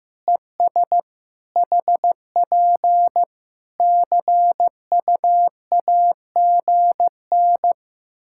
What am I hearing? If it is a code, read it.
Morse code, heard as a keyed tone, about 15 wpm, 700 Hz. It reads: ES HP CUAGN
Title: Practice QSO - Exchange 4 - Ending - 15wpm